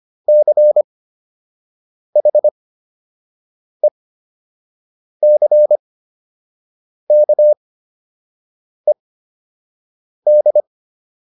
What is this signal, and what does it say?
Morse code, heard as a keyed tone, about 25 wpm, spaced as for 6 wpm, 600 Hz. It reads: CHECKED